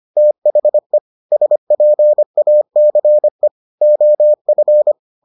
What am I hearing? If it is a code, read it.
Morse code, heard as a keyed tone, about 25 wpm, 600 Hz. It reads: THE SPACE OF